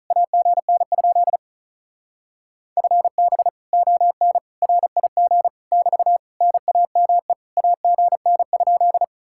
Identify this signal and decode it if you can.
Morse code, heard as a keyed tone, about 45 wpm, 700 Hz. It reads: AGN? FB ON RIG <BT> NAME AGN?